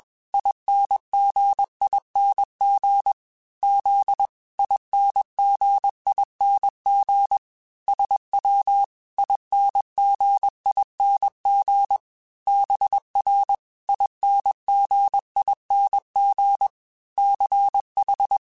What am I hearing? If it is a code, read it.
Morse code, heard as a keyed tone, about 21 wpm, 800 Hz. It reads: INGING Z INGING SW INGING BR INGING CH
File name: Special INGs / Special INGs-mp3s / Dble Ings sp @20